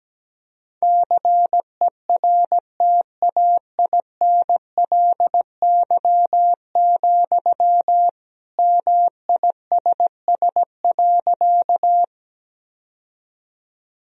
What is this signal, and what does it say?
Morse code, heard as a keyed tone, about 17 wpm, 700 Hz. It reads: CERTAINLY, MISS.
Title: Sentences from Top 1000 Words - Review Part 3 - 17wpm